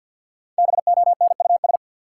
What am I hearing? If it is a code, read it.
Morse code, heard as a keyed tone, about 45 wpm, 700 Hz. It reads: BONUS
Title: Five Letter Words - 50wpm